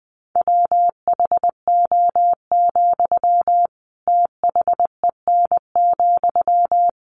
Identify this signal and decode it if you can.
Morse code, heard as a keyed tone, about 20 wpm, 700 Hz. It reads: WHO, THEN,
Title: Sentences from Top 100 Words 20wpm